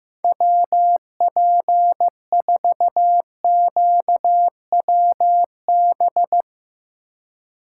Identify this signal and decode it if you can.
Morse code, heard as a keyed tone, about 16 wpm, 700 Hz. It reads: WP4QWB
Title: Speed Racing - USA 2x3 Call Signs 15wpm